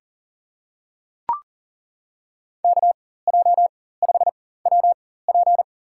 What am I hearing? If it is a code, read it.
Morse code, heard as a keyed tone, about 40 wpm, spaced as for 19 wpm, 700 Hz. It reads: KJ5WP
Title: USA Call Signs - 5 Characters - 4x Character Spacing - 40wpm